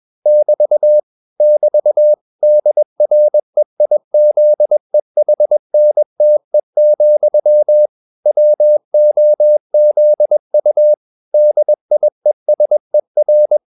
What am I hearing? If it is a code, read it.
Morse code, heard as a keyed tone, about 21 wpm, 600 Hz. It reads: <BT> <BT> DREIZEHNTE, WOZU DIESER
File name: marienkind2 21wpm